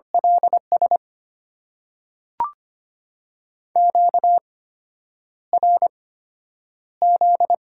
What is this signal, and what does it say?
Morse code, heard as a keyed tone, about 25 wpm, spaced as for 9 wpm, 700 Hz. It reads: LS QRZ
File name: Top 90 QSO Elements - 8x Character Spacing - 25wpm